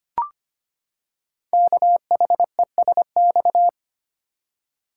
Morse code, heard as a keyed tone, about 25 wpm, 700 Hz. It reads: KHESX
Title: Code Groups - 5 Letters - 25wpm